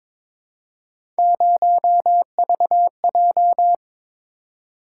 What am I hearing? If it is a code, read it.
Morse code, heard as a keyed tone, about 22 wpm, 700 Hz. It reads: 0VJ
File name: practice / 22wpm / Three Letter-Number Combinations - 22wpm